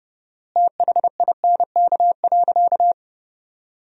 Morse code, heard as a keyed tone, about 30 wpm, 700 Hz. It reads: THINK.
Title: 4x Word-Spacing - Sentences from Top 500 Words - 30wpm